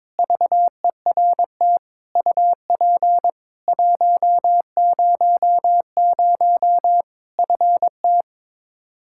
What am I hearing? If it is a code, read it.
Morse code, heard as a keyed tone, about 22 wpm, 700 Hz. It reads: VERT UP 100 FT
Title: Practice QSO - Exchange 2 - Info - 22wpm